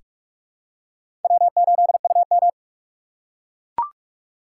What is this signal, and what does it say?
Morse code, heard as a keyed tone, about 44 wpm, 690 Hz. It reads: W8UM
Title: USA Call Signs - 4 Characters - 2x Character Spacing - 45wpm